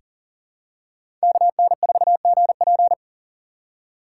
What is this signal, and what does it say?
Morse code, heard as a keyed tone, about 39 wpm, 700 Hz. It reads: KN4GP
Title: USA Call Signs - 4x Character Spacing - 40wpm